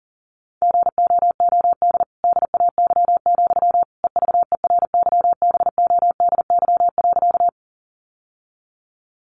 Morse code, heard as a keyed tone, about 40 wpm, 700 Hz. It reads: GOOD DAY, EVERYBODY.
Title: Sentences from Top 1400 Words 40wpm